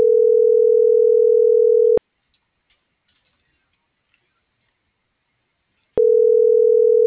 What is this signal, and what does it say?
Ringback tone (North America), 440 + 480 Hz.